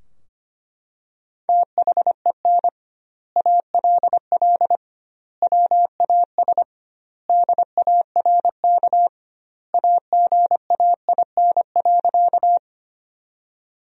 Morse code, heard as a keyed tone, about 25 wpm, 700 Hz. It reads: THEN ALL WAS DARK AGAIN.
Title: 2x Word-Spacing - Sentences from Top 500 Words - 25wpm